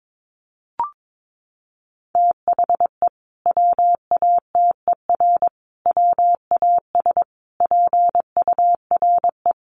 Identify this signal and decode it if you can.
Morse code, heard as a keyed tone, about 22 wpm, 700 Hz. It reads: THE WATER WAS PURE
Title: Sentences from Top 1500 Words 22wpm